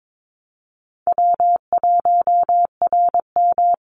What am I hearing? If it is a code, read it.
Morse code, heard as a keyed tone, about 22 wpm, 700 Hz. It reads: W1RM
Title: Easy Call Signs - 22wpm